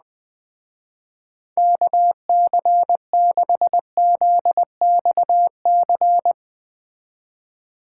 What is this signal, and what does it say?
Morse code, heard as a keyed tone, about 20 wpm, 700 Hz. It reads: KC6ZXC
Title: Speed Racing - USA 2x3 Call Signs 17wpm